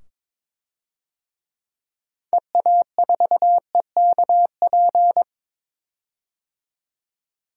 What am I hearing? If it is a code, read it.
Morse code, heard as a keyed tone, about 22 wpm, 700 Hz. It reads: EA4EKP
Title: DXCC-100 Call Signs - 6 Characters - 22wpm